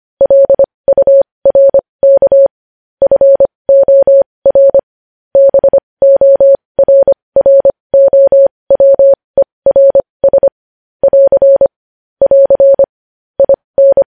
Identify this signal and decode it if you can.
Morse code, heard as a keyed tone, about 25 wpm, spaced as for 20 wpm, 550 Hz. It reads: LURK FOR BORROWERS <AR> <AR> IN
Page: news headlines in morse code at 20 wpm